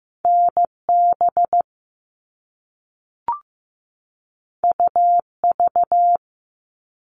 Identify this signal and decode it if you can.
Morse code, heard as a keyed tone, about 15 wpm, 700 Hz. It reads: NB UV
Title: All Cut Numbers - 2-Digit 15wpm